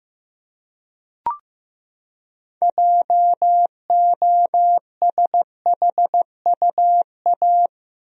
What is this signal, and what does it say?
Morse code, heard as a keyed tone, about 15 wpm, 700 Hz. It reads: JOSHUA
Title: USA Common Names - 15wpm